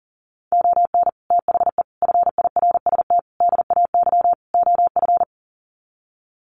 Morse code, heard as a keyed tone, about 40 wpm, 700 Hz. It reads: ON THE FIRST DAY OF
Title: Sets of 5 Words Encore 40wpm